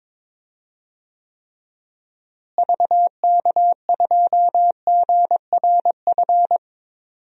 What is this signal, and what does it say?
Morse code, heard as a keyed tone, about 22 wpm, 700 Hz. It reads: VK2GRF
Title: DXCC-100 Call Signs - 6 Characters - Repeated 3x - 22wpm